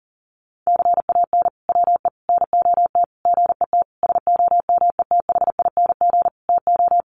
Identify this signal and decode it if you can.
Morse code, heard as a keyed tone, about 40 wpm, 700 Hz. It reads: CAN WE NOT GET SOMETHING TO
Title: Sentences from Top 1500 Words 40wpm